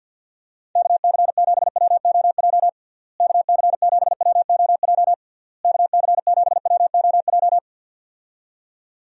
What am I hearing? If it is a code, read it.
Morse code, heard as a keyed tone, about 50 wpm, 700 Hz. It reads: KC7WOJ KC7WOJ KC7WOJ
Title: USA Call Signs - 6 Characters - Repeated 3x - 50wpm